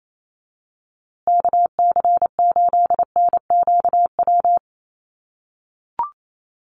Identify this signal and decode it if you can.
Morse code, heard as a keyed tone, about 28 wpm, 700 Hz. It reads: KC8NQW
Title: USA Call Signs Repeated 3x 28wpm